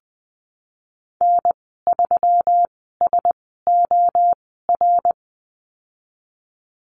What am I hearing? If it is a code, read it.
Morse code, heard as a keyed tone, about 20 wpm, spaced as for 15 wpm, 700 Hz. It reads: N3SOR
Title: 161 Lesson 26g - Letter g - Call Signs 20wpm